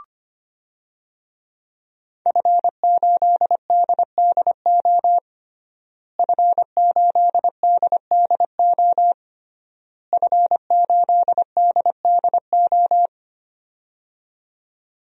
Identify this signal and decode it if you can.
Morse code, heard as a keyed tone, about 25 wpm, 700 Hz. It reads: F8DDO F8DDO F8DDO
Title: DXCC-100 Call Signs - 5 Characters - Repeated 3x - 25wpm